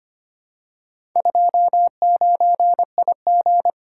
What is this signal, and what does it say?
Morse code, heard as a keyed tone, about 25 wpm, 700 Hz. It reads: 29IG
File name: practice / 25wpm / Four Letter-Number Combinations - 25wpm